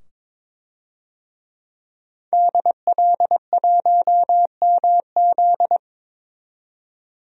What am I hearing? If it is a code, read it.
Morse code, heard as a keyed tone, about 22 wpm, 700 Hz. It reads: DL1MZ
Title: DXCC-100 Call Signs - 5 Characters - 22wpm